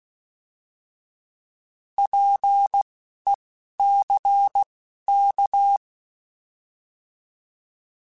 Morse code, heard as a keyed tone, about 16 wpm, spaced as for 11 wpm, 800 Hz. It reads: PECK
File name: Alphabetical / Alphabetical mp3s / Ps @10